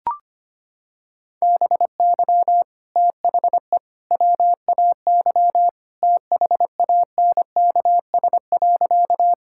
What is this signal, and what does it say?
Morse code, heard as a keyed tone, about 25 wpm, 700 Hz. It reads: BY THE WAY THANKS.
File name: practice / 25wpm / Sentences with 3-Word Phrases - 25wpm